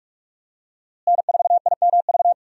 Morse code, heard as a keyed tone, about 47 wpm, 700 Hz. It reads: N4IMV
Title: USA Call Signs - 5 Characters - Repeated 2x - 45wpm